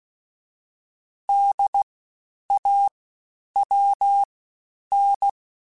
Morse code, heard as a keyed tone, about 16 wpm, 800 Hz. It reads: D A W N
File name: Special INGs-mp3s / Special INGs-mp3s / SlowToFaster ing @15